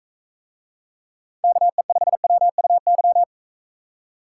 Morse code, heard as a keyed tone, about 40 wpm, 700 Hz. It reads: KE5WUY